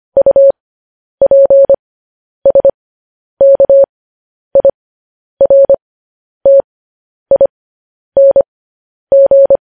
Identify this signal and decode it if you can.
Morse code, heard as a keyed tone, about 25 wpm, spaced as for 10 wpm, 550 Hz. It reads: UPSKIRTING
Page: News Headlines in Morse Code at 10 WPM